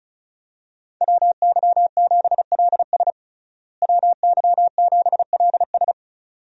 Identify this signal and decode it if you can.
Morse code, heard as a keyed tone, about 34 wpm, 700 Hz. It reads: WY7LS WY7LS
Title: USA Call Signs - 5 Characters - Repeated 2x - 35wpm